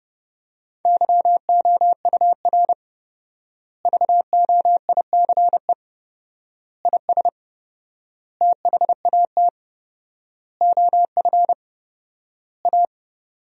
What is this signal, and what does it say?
Morse code, heard as a keyed tone, about 32 wpm, 700 Hz. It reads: YOUR VOICE IS THAT OF A
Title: 4x Word-Spacing - Sentences from Top 500 Words - 30wpm